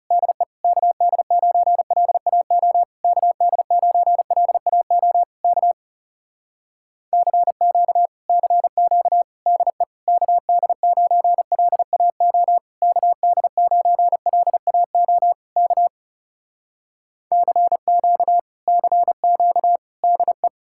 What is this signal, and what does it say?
Morse code, heard as a keyed tone, about 34 wpm, 700 Hz. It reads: DE KD9LAO KD9LAO K CQ CQ DE KD9LAO KD9LAO K CQ CQ DE
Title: Speed-Racing Common QSO Phrases - 30wpm